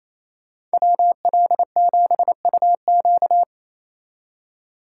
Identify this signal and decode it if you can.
Morse code, heard as a keyed tone, about 28 wpm, 700 Hz. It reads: WL7UQ
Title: Speed Racing - USA 2x2 Call Signs 28wpm